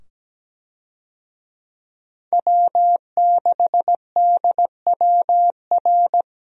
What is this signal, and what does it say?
Morse code, heard as a keyed tone, about 17 wpm, 700 Hz. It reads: W6DWR
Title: USA Call Signs - 5 Characters - 17wpm